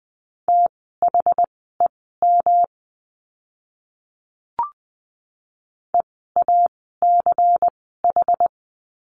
Morse code, heard as a keyed tone, about 20 wpm, spaced as for 15 wpm, 700 Hz. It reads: THEM EACH
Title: 192 Lesson 31f - Letter v - Words 20wpm